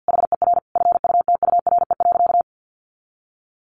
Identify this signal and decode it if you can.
Morse code, heard as a keyed tone, about 50 wpm, 700 Hz. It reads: HER FUTURE.
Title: Sentences from Top 1200 Words 50wpm